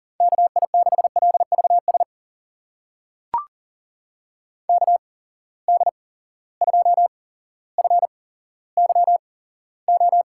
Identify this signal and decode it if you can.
Morse code, heard as a keyed tone, about 40 wpm, spaced as for 17 wpm, 700 Hz. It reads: KI6LVS KD2FYO
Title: USA Call Signs - 6 Characters - 8x Character Spacing - 40wpm